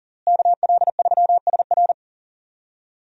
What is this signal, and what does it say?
Morse code, heard as a keyed tone, about 40 wpm, 700 Hz. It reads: KL3SR
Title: USA Call Signs - 5 Characters - Repeated 3x - 40wpm